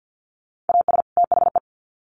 Morse code, heard as a keyed tone, about 50 wpm, 700 Hz. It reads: AS THE
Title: Sets of 2 Words 50wpm